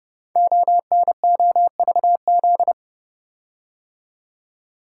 Morse code, heard as a keyed tone, about 29 wpm, 700 Hz. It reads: ONOVZ
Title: Code Groups - 5 Letters - 30wpm